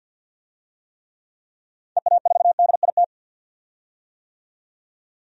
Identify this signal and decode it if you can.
Morse code, heard as a keyed tone, about 51 wpm, 700 Hz. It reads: EA4DIT
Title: DXCC-100 Call Signs - 6 Characters - 50wpm